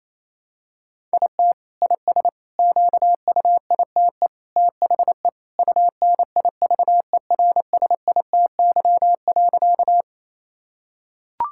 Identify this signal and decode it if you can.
Morse code, heard as a keyed tone, about 28 wpm, 700 Hz. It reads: IT IS QUITE THE UNIVERSITY.
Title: Sentences from Top 1000 Words - Review Part 5 - 28wpm